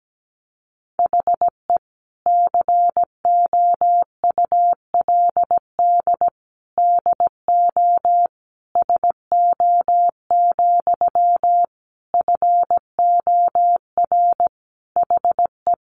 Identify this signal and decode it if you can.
Morse code, heard as a keyed tone, about 17 wpm, 700 Hz. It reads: HE COULD DO SO, FOR HE
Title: Sentences from Top 1500 Words 17wpm